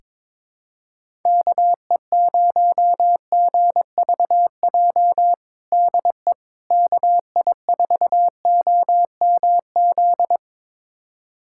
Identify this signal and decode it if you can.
Morse code, heard as a keyed tone, about 22 wpm, 700 Hz. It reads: KE0GVJ DE KI4OMZ
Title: USA Callsign DE Callsign 22wpm